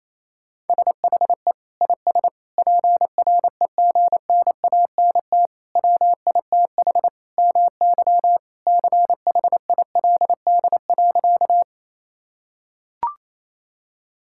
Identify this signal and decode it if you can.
Morse code, heard as a keyed tone, about 28 wpm, 700 Hz. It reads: SHE IS PREGNANT WITH MY CHILD.
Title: Sentences from Top 1000 Words - Review Part 6 - 28wpm